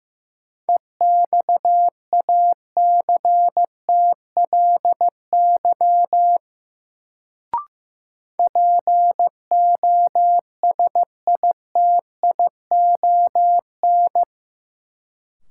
Morse code, heard as a keyed tone, about 15 wpm, 700 Hz. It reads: EXACTLY POSITION
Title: Sentences from Top 900 Words - New Word Drill - 15wpm